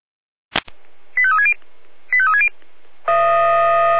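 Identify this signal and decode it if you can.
Five-tone selective call (ZVEI), heard as quick runs of five tones, tones of about 70 ms. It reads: 85360 85360
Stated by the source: Five-tone ZVEI selective call 85898, received on an SDR.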